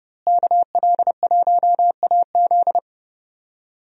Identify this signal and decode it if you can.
Morse code, heard as a keyed tone, about 32 wpm, 700 Hz. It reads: KL1AZ
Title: USA Call Signs - 5 Characters - Repeated 3x - 30wpm